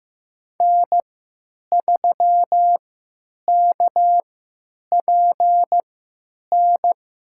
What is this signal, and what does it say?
Morse code, heard as a keyed tone, about 15 wpm, spaced as for 9 wpm, 700 Hz. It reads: N3KPN
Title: USA Call Signs - 5 Characters - 3x Character Spacing - 15wpm